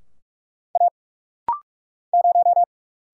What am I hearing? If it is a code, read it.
Morse code, heard as a keyed tone, about 46 wpm, 700 Hz. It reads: A 0
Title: Single Letter-Number - Rapid-Fire 45wpm